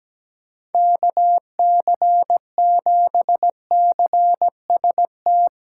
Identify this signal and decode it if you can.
Morse code, heard as a keyed tone, about 17 wpm, 700 Hz. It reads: KC7CST
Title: USA Call Signs - 6 Characters - 8x Character Spacing - 17wpm